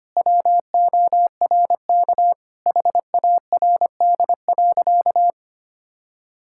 Morse code, heard as a keyed tone, about 25 wpm, 700 Hz. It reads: WORK HARD.
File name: practice / 25wpm / Sentences with 3-Word Phrases - 25wpm